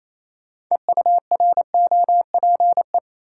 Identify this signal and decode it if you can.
Morse code, heard as a keyed tone, about 28 wpm, 700 Hz. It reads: EUROPE
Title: Sets of 5 Words Encore - New Word Drill - 28wpm